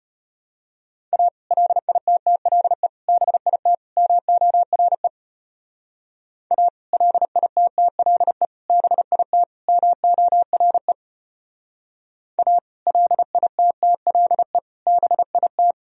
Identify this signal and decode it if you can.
Morse code, heard as a keyed tone, about 31 wpm, 700 Hz. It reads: A LITTLE BIT MORE A LITTLE BIT MORE A LITTLE BIT
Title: Speed-Racing - Sets of 4 Words 25wpm